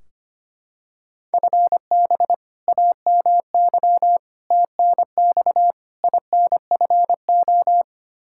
Morse code, heard as a keyed tone, about 25 wpm, 700 Hz. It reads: FB AMY TNX INFO
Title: Practice QSO - Exchange 3 - More Info - 25wpm